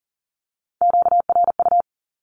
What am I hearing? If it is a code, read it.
Morse code, heard as a keyed tone, about 40 wpm, 700 Hz. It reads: QRU
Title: Q-Codes Repeated 3x 40wpm